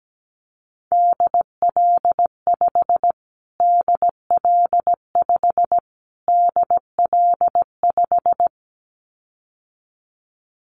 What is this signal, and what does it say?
Morse code, heard as a keyed tone, about 17 wpm, 700 Hz. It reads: DL5 DL5 DL5
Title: Callsign Prefixes for Top 25 DXCC - Repeated 3x - 17wpm